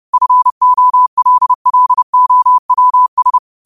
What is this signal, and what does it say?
Morse code, heard as a keyed tone, about 30 wpm, 1000 Hz. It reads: FOLLOWS
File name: 30wpm